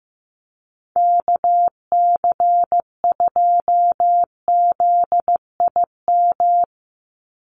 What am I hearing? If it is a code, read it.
Morse code, heard as a keyed tone, about 15 wpm, 700 Hz. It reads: KC2ZIM
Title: USA Call Signs 15wpm